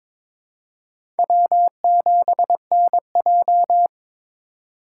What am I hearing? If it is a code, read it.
Morse code, heard as a keyed tone, about 22 wpm, 700 Hz. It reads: W7NJ
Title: USA Call Signs - 4 Characters - 3x Character Spacing - 22wpm